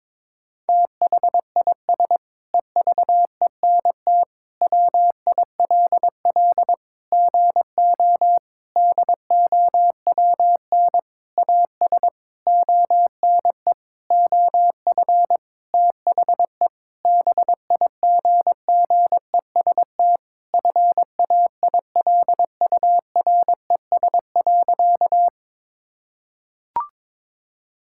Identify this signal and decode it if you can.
Morse code, heard as a keyed tone, about 22 wpm, 700 Hz. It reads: THIS EVENT WILL GO DOWN AS ONE OF THE BIGGEST FAILURES.
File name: Sentences with 7-Word Phrases - 22wpm